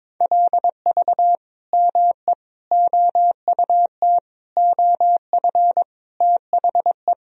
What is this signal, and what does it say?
Morse code, heard as a keyed tone, about 22 wpm, 700 Hz. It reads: LV ME OUT OF THE
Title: Sentences from Top 400 Words and Abbreviations 22wpm